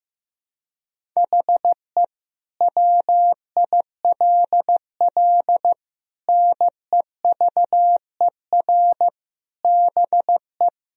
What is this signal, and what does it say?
Morse code, heard as a keyed tone, about 15 wpm, 700 Hz. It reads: HE WILL NEVER BE
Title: Sentences from Top 1000 Words - Review Part 2 - 15wpm